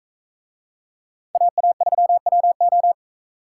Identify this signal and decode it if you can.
Morse code, heard as a keyed tone, about 43 wpm, 700 Hz. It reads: AA3WO